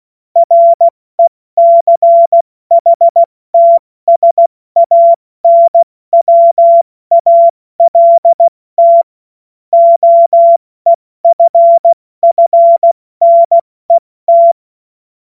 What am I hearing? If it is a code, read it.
Morse code, heard as a keyed tone, about 16 wpm, 680 Hz. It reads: RECHTSANWALT OEFFNET
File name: Morsekurs / Witze / Witz 02. August 2022 70 F80